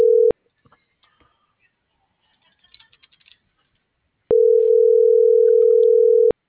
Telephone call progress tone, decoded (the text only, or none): ringback tone